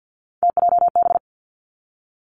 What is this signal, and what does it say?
Morse code, heard as a keyed tone, about 50 wpm, 700 Hz. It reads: TJB
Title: Three Challenging Letter-Number Combinations 50wpm